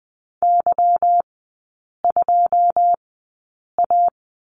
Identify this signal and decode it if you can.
Morse code, heard as a keyed tone, about 20 wpm, spaced as for 8 wpm, 700 Hz. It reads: Y2A
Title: 152 Lesson 25e - Letter p - Three Characters 20wpm